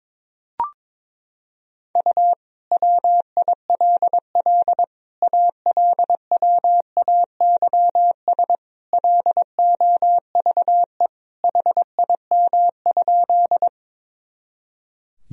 Morse code, heard as a keyed tone, about 22 wpm, 700 Hz. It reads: U WILL ALWAYS LOVE HIM?
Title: Sentences from Top 400 Words and Abbreviations 22wpm